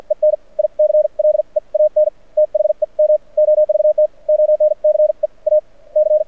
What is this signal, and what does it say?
Morse code, heard as a keyed tone, about 52 wpm, spaced as for 43 wpm, 590 Hz. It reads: EN I CLEAN THEM OVT ONCE A C
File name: archived / CRSnet